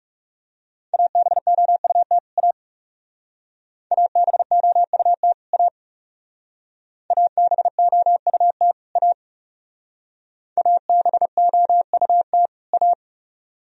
Morse code, heard as a keyed tone, about 37 wpm, 700 Hz. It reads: ABOUT A ABOUT A ABOUT A ABOUT A
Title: Speed-Racing - Sets of 2 Words 30wpm